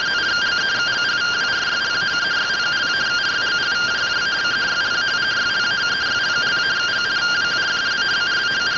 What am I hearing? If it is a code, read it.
Two-tone teleprinter signal, about 170 Hz shift.